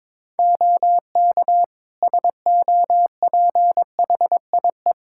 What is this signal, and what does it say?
Morse code, heard as a keyed tone, about 22 wpm, 700 Hz. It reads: OK SOPHIE
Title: Practice QSO - Exchange 2 - Info - 22wpm